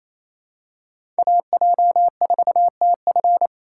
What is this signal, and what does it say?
Morse code, heard as a keyed tone, about 28 wpm, 700 Hz. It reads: AJ4TF
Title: K1USN SST QSO Practice - 28wpm